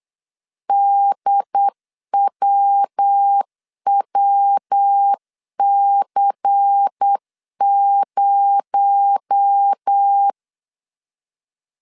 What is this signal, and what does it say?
Morse code, heard as a keyed tone, about 8 wpm, 800 Hz. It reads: DWWC0
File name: CW / DET / 1 6 1500C S C 8wpm 5219